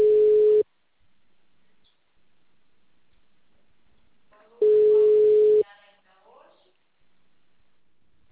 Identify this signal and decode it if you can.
Ringback tone (Europe, 425 Hz), with speech.